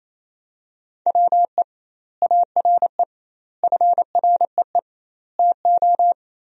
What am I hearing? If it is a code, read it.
Morse code, heard as a keyed tone, about 28 wpm, 700 Hz. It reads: WE ARE FREE TO
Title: 2x Word-Spacing - Sentences from Top 500 Words - 28wpm